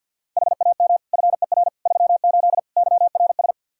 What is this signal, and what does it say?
Morse code, heard as a keyed tone, about 52 wpm, 700 Hz. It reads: HAM FER 38 YRS
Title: K1USN SST QSO Practice 50wpm